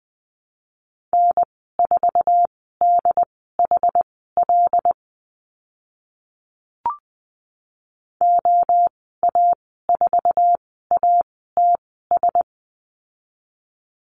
Morse code, heard as a keyed tone, about 20 wpm, spaced as for 15 wpm, 700 Hz. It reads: N4DHL OA4ATS
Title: 076 Lesson 13g - Letter L - Call Signs 20wpm